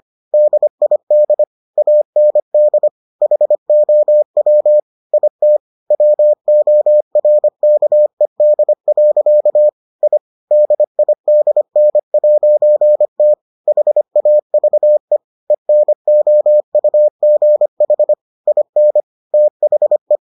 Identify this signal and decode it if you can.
Morse code, heard as a keyed tone, about 25 wpm, 600 Hz. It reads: DID AND HOW IT WORKED. I DIDN'T HAVE ENOUGH IN THE